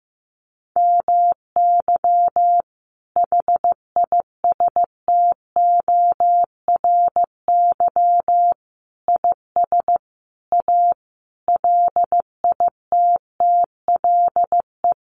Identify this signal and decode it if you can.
Morse code, heard as a keyed tone, about 14 wpm, 700 Hz. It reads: MY HISTORY IS A LITTLE
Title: Sentences from Top 1500 Words 15wpm